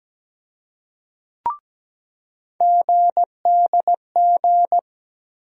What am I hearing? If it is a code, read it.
Morse code, heard as a keyed tone, about 17 wpm, 700 Hz. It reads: GDG
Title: USA Call Sign Postfixes - 3 Letters 17wpm